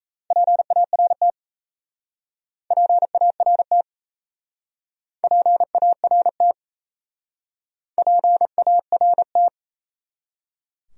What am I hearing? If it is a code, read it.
Morse code, heard as a keyed tone, about 36 wpm, 700 Hz. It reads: PART PART PART PART